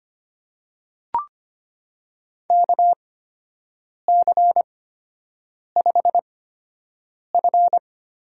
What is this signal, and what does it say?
Morse code, heard as a keyed tone, about 25 wpm, spaced as for 7 wpm, 700 Hz. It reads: KC5F
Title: USA Call Signs - 8x Character Spacing - 25wpm